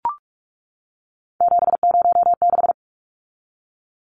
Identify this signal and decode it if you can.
Morse code, heard as a keyed tone, about 44 wpm, 700 Hz. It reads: 706
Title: Numbers - 3-Digit 45wpm